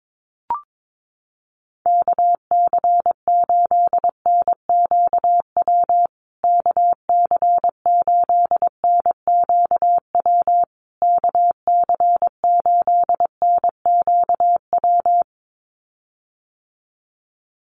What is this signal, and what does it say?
Morse code, heard as a keyed tone, about 22 wpm, 690 Hz. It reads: KC8NQW KC8NQW KC8NQW
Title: USA Call Signs Repeated 3x 22wpm